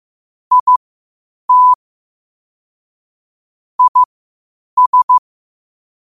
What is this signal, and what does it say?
Morse code, heard as a keyed tone, about 15 wpm, spaced as for 8 wpm, 1000 Hz. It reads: IT IS
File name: qotdfiles / 8wpm